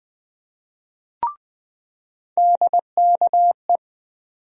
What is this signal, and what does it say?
Morse code, heard as a keyed tone, about 20 wpm, 700 Hz. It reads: DKE
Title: USA Call Sign Postfixes - 3 Letters 20wpm